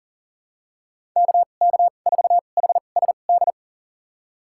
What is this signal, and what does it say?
Morse code, heard as a keyed tone, about 37 wpm, spaced as for 28 wpm, 690 Hz. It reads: KK4HSD
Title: USA Call Signs - 2x Character Spacing - 40wpm